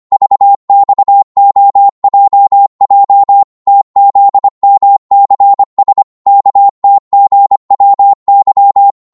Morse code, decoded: VXOJJ TZMCS KTGWY